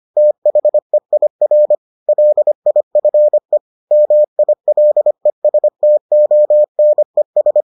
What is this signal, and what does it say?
Morse code, heard as a keyed tone, about 25 wpm, 600 Hz. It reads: THEIR LIFE MILESTONES